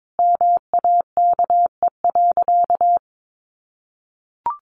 Morse code, heard as a keyed tone, about 22 wpm, 700 Hz. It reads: MAKE.
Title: Sentences from Top 1100 Words 22wpm